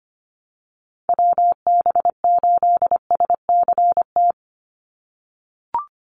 Morse code, heard as a keyed tone, about 25 wpm, 700 Hz. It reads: WB8SCT
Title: Call Signs - 25wpm